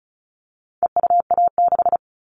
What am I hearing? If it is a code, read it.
Morse code, heard as a keyed tone, about 34 wpm, 700 Hz. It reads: EUA6
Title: All Cut Numbers - 4-Digit 35wpm